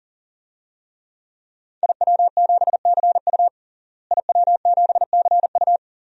Morse code, heard as a keyed tone, about 40 wpm, 700 Hz. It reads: IW7CU IW7CU
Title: DXCC-100 Call Signs - 5 Characters - Repeated 2x - 40wpm